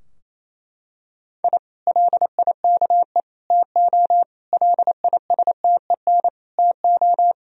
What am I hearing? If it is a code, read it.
Morse code, heard as a keyed tone, about 28 wpm, 700 Hz. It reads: I LIKE TO LISTEN TO
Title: Sentences from Top 1000 Words - Review Part 3 - 28wpm